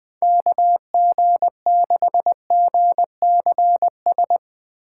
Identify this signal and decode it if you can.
Morse code, heard as a keyed tone, about 20 wpm, 700 Hz. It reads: KG6GCS
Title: USA Call Signs - 6 Characters - 2x Character Spacing - 20wpm